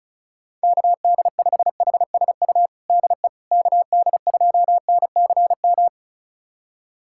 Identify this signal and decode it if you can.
Morse code, heard as a keyed tone, about 34 wpm, 690 Hz. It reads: KD5HSU DE KD2NCM
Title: USA Callsign DE Callsign - 6 Characters - 35wpm